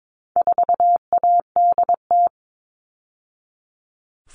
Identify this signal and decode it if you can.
Morse code, heard as a keyed tone, about 22 wpm, 700 Hz. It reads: 4ADT
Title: All Cut Numbers - 4-Digit 22wpm